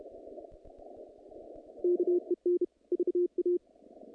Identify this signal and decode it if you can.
Morse code, heard as a keyed tone, about 32 wpm, 350 Hz. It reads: KEN VA
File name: contests / naqp-cw-k5tr-op-k5pi- -2300